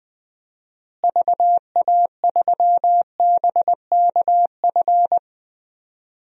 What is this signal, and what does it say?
Morse code, heard as a keyed tone, about 21 wpm, 700 Hz. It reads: VA3BKF
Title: DXCC-100 Call Signs - 6 Characters - Repeated 3x - 20wpm